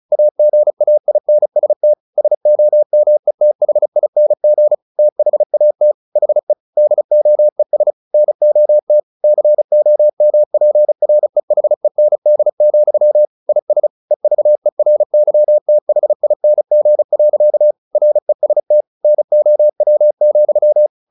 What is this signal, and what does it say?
Morse code, heard as a keyed tone, about 34 wpm, 600 Hz. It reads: AGAINST SOMETHING THAT HE DOES NOT COMPREHEND, IS EVERYTHING. REST NOW,
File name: the old man and the sea 2 35wpm